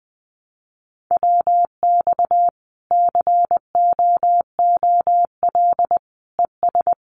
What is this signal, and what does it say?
Morse code, heard as a keyed tone, about 20 wpm, 700 Hz. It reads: WX COOL ES
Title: Common QSO Phrases 20wpm